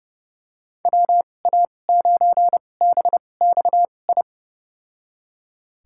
Morse code, heard as a keyed tone, about 30 wpm, spaced as for 22 wpm, 700 Hz. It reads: WA9BXI
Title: USA Call Signs - 6 Characters - 2x Character Spacing - 30wpm